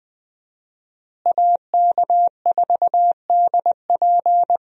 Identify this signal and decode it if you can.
Morse code, heard as a keyed tone, about 20 wpm, 700 Hz. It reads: AK4DP
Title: USA Call Signs - 5 Characters - Repeated 3x - 20wpm